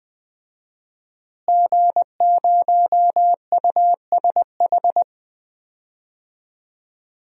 Morse code, heard as a keyed tone, about 20 wpm, 700 Hz. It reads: G0USH
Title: DXCC-100 Call Signs - 5 Characters - Repeated 2x - 20wpm